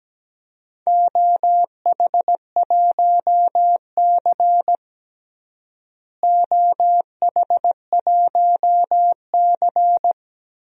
Morse code, heard as a keyed tone, about 17 wpm, 700 Hz. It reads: OH1C OH1C
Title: DXCC-100 Call Signs - 4 Characters - Repeated 3x - 17wpm